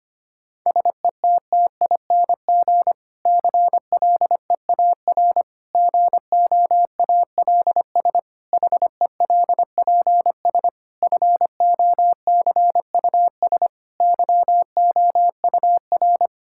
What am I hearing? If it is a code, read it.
Morse code, heard as a keyed tone, about 25 wpm, 700 Hz. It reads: SETTING CLEAR GOALS HELPS FOCUS YOUR
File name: practice / 25wpm / Difficult Sentences with Hints - 25wpm